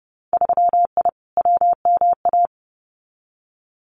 Morse code, heard as a keyed tone, about 30 wpm, 700 Hz. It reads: 3I WMA
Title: Winter Field Day Exchanges Encore 30wpm